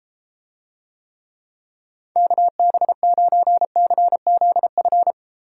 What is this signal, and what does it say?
Morse code, heard as a keyed tone, about 30 wpm, 700 Hz. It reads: KB9CZF